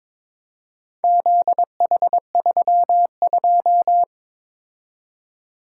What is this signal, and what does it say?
Morse code, heard as a keyed tone, about 23 wpm, 700 Hz. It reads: ZH32
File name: practice / 22wpm / Four Letter-Number Combinations - 22wpm